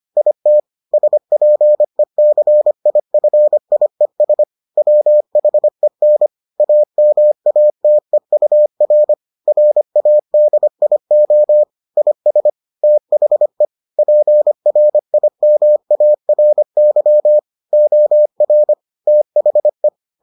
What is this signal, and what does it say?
Morse code, heard as a keyed tone, about 25 wpm, 600 Hz. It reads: IT SPECIFIES WHEN AMATEUR RADIO IS THE PRIMARY OR THE